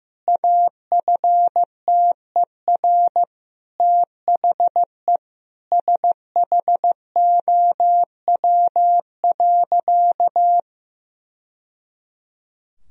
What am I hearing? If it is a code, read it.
Morse code, heard as a keyed tone, about 15 wpm, 700 Hz. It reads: AFTER THE SHOW.
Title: Sentences from Top 1000 Words - Review Part 3 - 15wpm